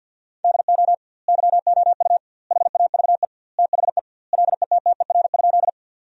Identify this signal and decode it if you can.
Morse code, heard as a keyed tone, about 50 wpm, 700 Hz. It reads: DO YOU HAVE THE LETTER?
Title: 2x Word-Spacing - Sentences from Top 300 Words - 50wpm